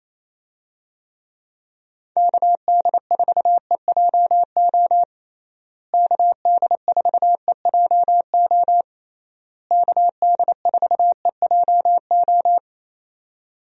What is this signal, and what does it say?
Morse code, heard as a keyed tone, about 28 wpm, 700 Hz. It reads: KD4EJO KD4EJO KD4EJO
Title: USA Call Signs - 6 Characters - Repeated 3x - 28wpm